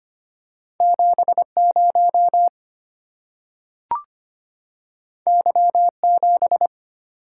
Morse code, heard as a keyed tone, about 25 wpm, 700 Hz. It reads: 70 Y7
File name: Two Letter-Number Combinations - 25wpm